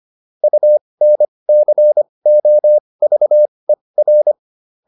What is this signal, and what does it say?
Morse code, heard as a keyed tone, about 25 wpm, spaced as for 20 wpm, 600 Hz. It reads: UNCOVER